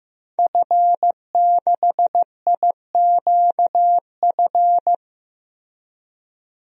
Morse code, heard as a keyed tone, about 15 wpm, 700 Hz. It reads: F6IQF
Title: DXCC-100 Call Signs - 5 Characters - Repeated 2x - 15wpm